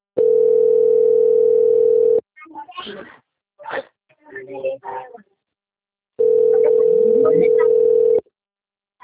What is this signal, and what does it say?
Ringback tone (North America), 440 + 480 Hz, with speech.
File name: out- - - - - .3764